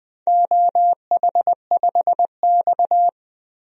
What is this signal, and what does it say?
Morse code, heard as a keyed tone, about 21 wpm, 700 Hz. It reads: OH5X
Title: DXCC-100 Call Signs - 4 Characters - Repeated 2x - 20wpm